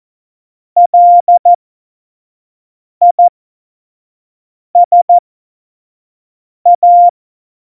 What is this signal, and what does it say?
Morse code, heard as a keyed tone, about 14 wpm, spaced as for 5 wpm, 700 Hz. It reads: LISA
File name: Morsekurs / Witze / Witz 06. Oktober 2020, 25 F70